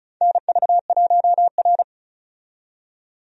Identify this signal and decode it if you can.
Morse code, heard as a keyed tone, about 36 wpm, 700 Hz. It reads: NV1R
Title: USA Call Signs - 4 Characters - 35wpm